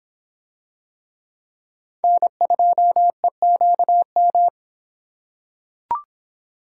Morse code, heard as a keyed tone, about 25 wpm, 700 Hz. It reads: N2EQM